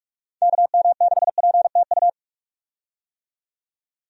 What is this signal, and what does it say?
Morse code, heard as a keyed tone, about 46 wpm, 700 Hz. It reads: KM6PTU
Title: USA Call Signs - 6 Characters - 45wpm